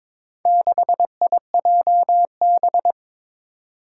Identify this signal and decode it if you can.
Morse code, heard as a keyed tone, about 22 wpm, 700 Hz. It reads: 6IJB
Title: Four Challenging Letter-Number Combinations 22wpm